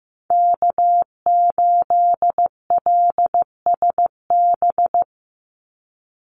Morse code, heard as a keyed tone, about 14 wpm, 700 Hz. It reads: K8LSB
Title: Call Signs Repeated 3x - 15wpm